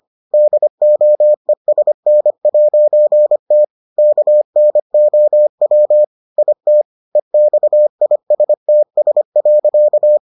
Morse code, heard as a keyed tone, about 25 wpm, 600 Hz. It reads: DOESN'T KNOW IT EXISTS.